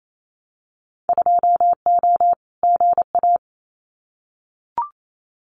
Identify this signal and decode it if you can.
Morse code, heard as a keyed tone, about 28 wpm, 700 Hz. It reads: 2O GA
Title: Winter Field Day Exchanges Encore 28wpm